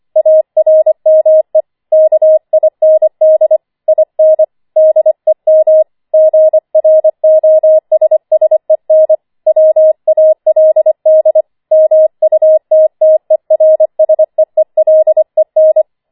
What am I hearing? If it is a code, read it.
Morse code, heard as a keyed tone, about 25 wpm, 630 Hz. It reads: ARME KIND IN DEM GROSSEN WALD MUTTERSEELEN
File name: 25wpm